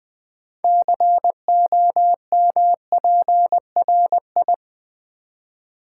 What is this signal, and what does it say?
Morse code, heard as a keyed tone, about 20 wpm, 700 Hz. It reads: COMPRI
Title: Build A Word - 2 Letters At A Time - 8-Letter Words - 20wpm